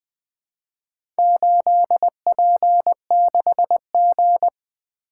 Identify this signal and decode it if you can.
Morse code, heard as a keyed tone, about 20 wpm, 700 Hz. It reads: 8P6G
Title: Four Letter-Number Combinations - 20wpm